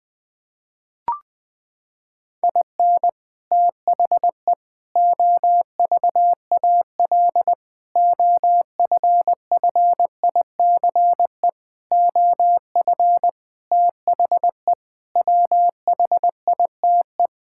Morse code, heard as a keyed tone, about 21 wpm, 700 Hz. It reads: IN THE OVAL OFFICE OF THE WHITE
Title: Sets of 8 Words - 20wpm